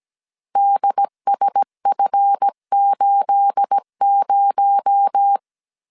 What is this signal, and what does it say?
Morse code, heard as a keyed tone, about 17 wpm, 800 Hz. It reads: DSF80
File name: CW / DET / 1 9 1500C S A 16wpm 2828